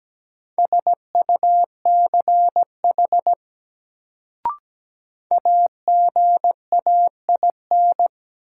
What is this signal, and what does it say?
Morse code, heard as a keyed tone, about 17 wpm, 700 Hz. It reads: SUCH AGAIN
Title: Sentences from Top 200 Words - New Word Drill - 17wpm